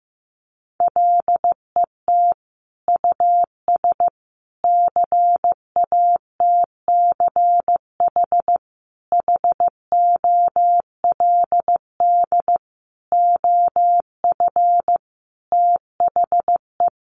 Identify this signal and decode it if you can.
Morse code, heard as a keyed tone, about 14 wpm, 700 Hz. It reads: LET US CATCH HOLD OF THE
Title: Sentences from Top 1500 Words 15wpm